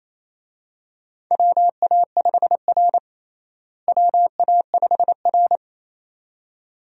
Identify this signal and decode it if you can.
Morse code, heard as a keyed tone, about 27 wpm, 700 Hz. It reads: WA5R WA5R
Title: USA Call Signs - 4 Characters - Repeated 2x - 28wpm